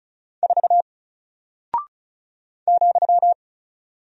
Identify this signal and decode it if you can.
Morse code, heard as a keyed tone, about 34 wpm, 700 Hz. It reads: 4 ,
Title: Single Character (Letter:Number:Punctuation:Prosign) Alt 35wpm